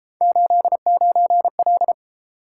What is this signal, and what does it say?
Morse code, heard as a keyed tone, about 31 wpm, 700 Hz. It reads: 89L